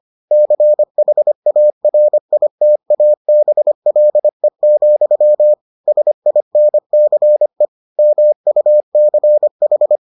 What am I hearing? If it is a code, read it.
Morse code, heard as a keyed tone, about 25 wpm, 600 Hz. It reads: CHARITABLE, SINCE MUCH